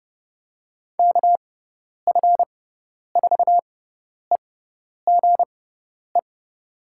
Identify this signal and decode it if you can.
Morse code, heard as a keyed tone, about 30 wpm, spaced as for 10 wpm, 700 Hz. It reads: KF4EGE
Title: USA Call Signs - 6 Characters - 6x Character Spacing - 30wpm